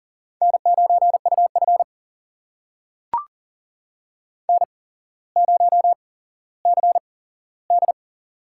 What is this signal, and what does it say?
Morse code, heard as a keyed tone, about 40 wpm, spaced as for 17 wpm, 700 Hz. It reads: N9UF N0CD
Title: USA Call Signs - 4 Characters - 8x Character Spacing - 40wpm